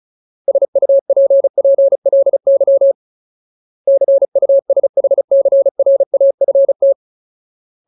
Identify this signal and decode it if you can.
Morse code, heard as a keyed tone, about 34 wpm, 550 Hz. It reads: SUPPLY CUSHCRAFT
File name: Ultimate Word List / 35WPM Ultimate